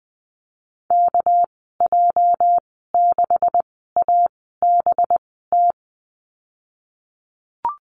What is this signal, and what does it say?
Morse code, heard as a keyed tone, about 20 wpm, spaced as for 14 wpm, 700 Hz. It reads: KJ6ABT
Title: 238 Lesson 38g - Letter q - Call Signs 20wpm